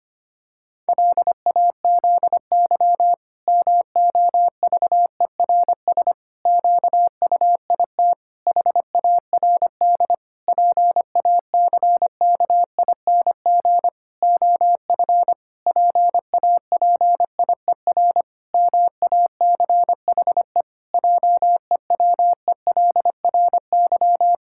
Morse code, heard as a keyed tone, about 25 wpm, 700 Hz. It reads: LAZY MOVERS QUIT HARD PACKING OF PAPIER MACHE JEWELRY
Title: Pangram Challenge v2 - 25wpm